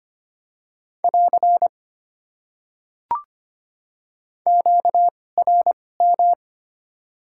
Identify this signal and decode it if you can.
Morse code, heard as a keyed tone, about 25 wpm, spaced as for 18 wpm, 700 Hz. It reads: <AR> QRM
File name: Top 90 QSO Elements - 2x Character Spacing - 25wpm